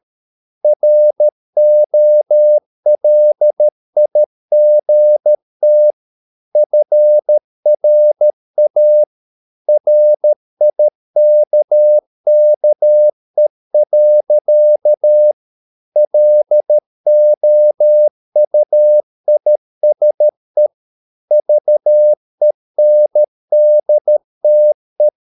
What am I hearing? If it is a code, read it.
Morse code, heard as a keyed tone, about 13 wpm, 600 Hz. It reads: ROLIGT FRA RIKKE. LOUISE VENDTE
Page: Ferie 13 wpm